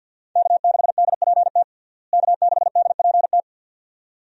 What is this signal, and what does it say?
Morse code, heard as a keyed tone, about 50 wpm, 700 Hz. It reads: K6DPT K6DPT
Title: USA Call Signs - 5 Characters - Repeated 2x - 50wpm